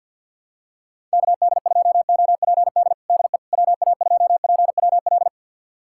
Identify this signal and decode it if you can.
Morse code, heard as a keyed tone, about 53 wpm, 700 Hz. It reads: KD2OPD DE WA2PWL
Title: USA Callsign DE Callsign 50wpm